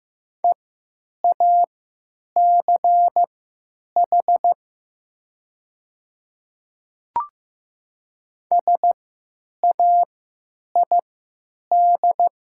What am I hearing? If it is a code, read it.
Morse code, heard as a keyed tone, about 15 wpm, spaced as for 9 wpm, 700 Hz. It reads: EACH SAID
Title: 3x Character Spacing - Top 100 Words - 15wpm